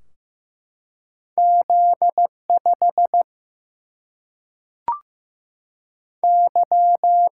Morse code, heard as a keyed tone, about 15 wpm, 700 Hz. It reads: Z 5 Y
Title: Single-Letter Single-Number 15wpm